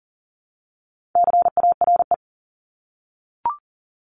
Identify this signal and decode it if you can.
Morse code, heard as a keyed tone, about 42 wpm, 700 Hz. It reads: CARE
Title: Multi-letter Phonemes 40wpm